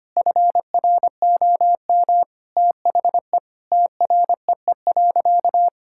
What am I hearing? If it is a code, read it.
Morse code, heard as a keyed tone, about 25 wpm, 700 Hz. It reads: FROM THE TREE.
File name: practice / 25wpm / Sentences with 2-Word Phrases - 25wpm